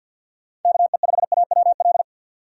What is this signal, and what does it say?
Morse code, heard as a keyed tone, about 46 wpm, 700 Hz. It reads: KE5AWL
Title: USA Call Signs - 6 Characters - 4x Character Spacing - 50wpm